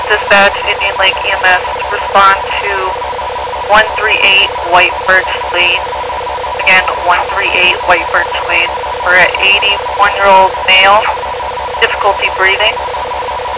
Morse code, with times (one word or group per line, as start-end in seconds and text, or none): none